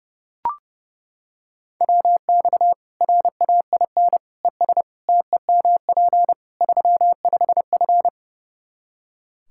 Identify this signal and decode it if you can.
Morse code, heard as a keyed tone, about 32 wpm, 700 Hz. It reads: WX RAIN ES TEMP 35F
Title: Practice QSO - Exchange 2 - Info - 30wpm